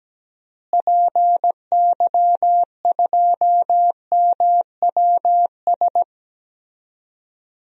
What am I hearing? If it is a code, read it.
Morse code, heard as a keyed tone, about 17 wpm, 700 Hz. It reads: PY2MWS
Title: DXCC-100 Call Signs - 6 Characters - Repeated 2x - 17wpm